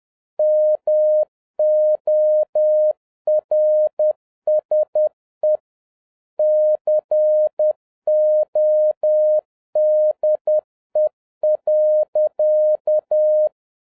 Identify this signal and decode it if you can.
Morse code, heard as a keyed tone, about 10 wpm, 610 Hz. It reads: MORSE CODE.